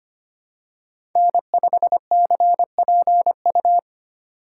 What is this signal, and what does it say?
Morse code, heard as a keyed tone, about 25 wpm, 700 Hz. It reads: N5CPU
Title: USA Call Signs - 5 Characters - 2x Character Spacing - 25wpm